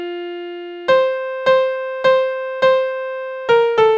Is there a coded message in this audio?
No.